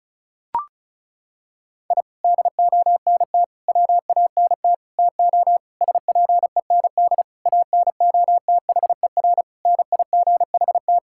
Morse code, heard as a keyed tone, about 34 wpm, 690 Hz. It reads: I DONT WANT TO SPEND ANOTHER NIGHT
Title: Sentences from Top 1000 Words - Review Part 5 - 35wpm